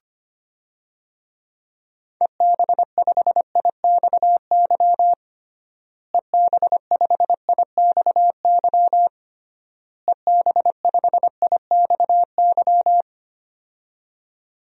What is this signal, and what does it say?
Morse code, heard as a keyed tone, about 25 wpm, 700 Hz. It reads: EB5IXY EB5IXY EB5IXY
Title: DXCC-100 Call Signs - 6 Characters - Repeated 3x - 25wpm